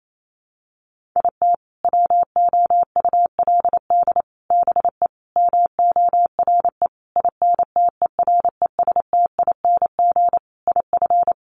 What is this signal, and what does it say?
Morse code, heard as a keyed tone, about 28 wpm, 700 Hz. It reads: IT WOULD BE MORE INTERESTING IF
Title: Sentences from Top 1000 Words 28wpm